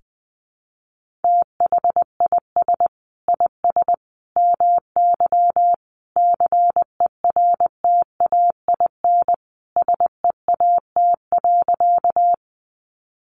Morse code, heard as a keyed tone, about 20 wpm, 700 Hz. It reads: THIS IS MY CERTAIN SEAT.
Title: Sentences from Top 1500 Words 20wpm